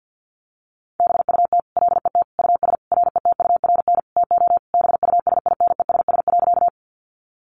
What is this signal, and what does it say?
Morse code, heard as a keyed tone, about 50 wpm, 700 Hz. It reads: BUT LET US RETURN TO BUSINESS.
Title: Sentences from Top 800 Words 50wpm